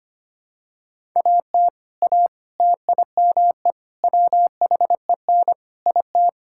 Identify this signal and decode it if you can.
Morse code, heard as a keyed tone, about 25 wpm, 700 Hz. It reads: AT A TIME WHEN IT
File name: practice / 25wpm / Sentences with 4-Word Phrases - 25wpm